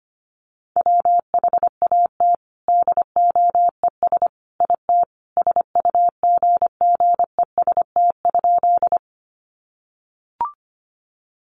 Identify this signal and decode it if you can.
Morse code, heard as a keyed tone, about 25 wpm, 700 Hz. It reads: WHAT DOES IT SUGGEST?
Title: Sentences from Top 1300 Words 25wpm